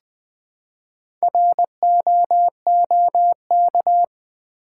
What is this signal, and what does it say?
Morse code, heard as a keyed tone, about 20 wpm, 700 Hz. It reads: ROOK
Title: Build A Word - 2 Letters At A Time - 6-Letter Words - 20wpm